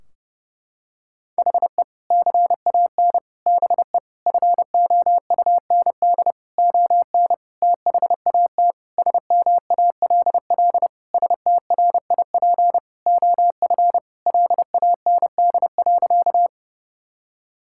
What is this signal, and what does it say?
Morse code, heard as a keyed tone, about 32 wpm, 700 Hz. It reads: HE CAN BE FOUND ON THAT SMALL STRIP OF LAND.
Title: Sentences from Top 1600 Words 30wpm